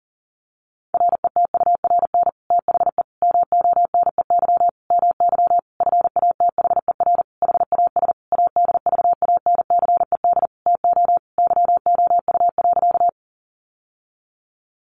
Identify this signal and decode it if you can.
Morse code, heard as a keyed tone, about 40 wpm, 700 Hz. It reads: RETURN THE MONEY MY FATHER HAS ADVANCED TO YOU.
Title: Sentences from Top 1200 Words 40wpm